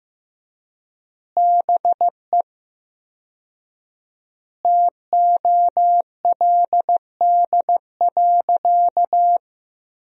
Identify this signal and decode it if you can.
Morse code, heard as a keyed tone, about 15 wpm, 700 Hz. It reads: BE TOLD.
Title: 4x Word-Spacing - Sentences from Top 400 Words - 15wpm